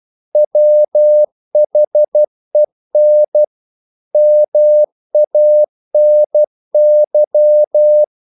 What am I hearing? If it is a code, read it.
Morse code, heard as a keyed tone, about 12 wpm, 600 Hz. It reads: WHEN MANY
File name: CW / Practice / EnglishQuotations0112